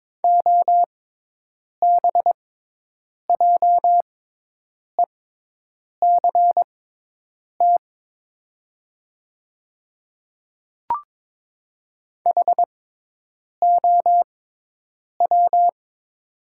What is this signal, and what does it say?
Morse code, heard as a keyed tone, about 22 wpm, spaced as for 8 wpm, 700 Hz. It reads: OBJECT HOW
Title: 6x Character Spacing - Top 500 Words - 22wpm